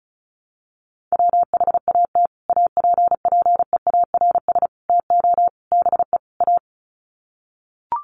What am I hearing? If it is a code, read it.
Morse code, heard as a keyed tone, about 34 wpm, 700 Hz. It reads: WHAT APPEARS TO BE A
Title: Sets of 5 Words Encore2 35wpm